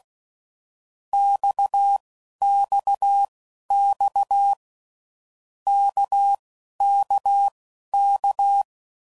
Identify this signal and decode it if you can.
Morse code, heard as a keyed tone, about 16 wpm, spaced as for 11 wpm, 800 Hz. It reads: XXX KKK
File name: Groups / Basic Letter Groups / RKPX